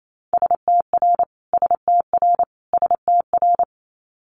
Morse code, heard as a keyed tone, about 28 wpm, 700 Hz. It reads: STR STR STR
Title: Multi-letter Phonemes Repeated 3x 28wpm